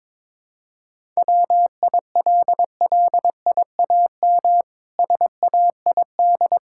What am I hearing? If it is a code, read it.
Morse code, heard as a keyed tone, about 22 wpm, 700 Hz. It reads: WILLIAM SAID
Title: Pangram Challenge v2 - 22wpm